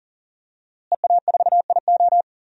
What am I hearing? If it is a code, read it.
Morse code, heard as a keyed tone, about 40 wpm, 700 Hz. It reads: EA4IO
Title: DXCC-100 Call Signs - 5 Characters - Repeated 2x - 40wpm